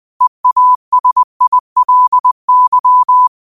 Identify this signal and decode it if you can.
Morse code, heard as a keyed tone, about 20 wpm, 1000 Hz. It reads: EASILY
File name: qotdfiles / 20wpm